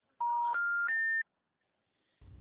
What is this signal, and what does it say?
Special information tone.